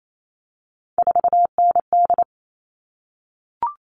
Morse code, heard as a keyed tone, about 28 wpm, 700 Hz. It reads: 4ND
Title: All Cut Numbers - 3-Digit 28wpm